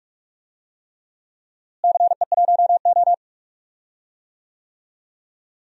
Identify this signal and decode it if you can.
Morse code, heard as a keyed tone, about 45 wpm, 700 Hz. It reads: CE1O
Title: DXCC-100 Call Signs - 4 Characters - Repeated 3x - 45wpm